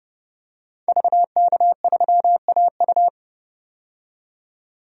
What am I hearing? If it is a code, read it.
Morse code, heard as a keyed tone, about 29 wpm, 700 Hz. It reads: VK3AU
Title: DXCC-100 Call Signs - 5 Characters - 30wpm